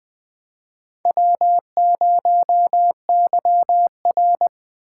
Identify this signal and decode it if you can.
Morse code, heard as a keyed tone, about 21 wpm, 700 Hz. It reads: W0YR
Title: K1USN SST QSO Practice - 20wpm